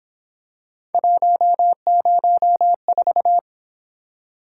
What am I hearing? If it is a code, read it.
Morse code, heard as a keyed tone, about 27 wpm, 700 Hz. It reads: 104
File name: Speed-Racing - Three Letter-Number Combinations - 22wpm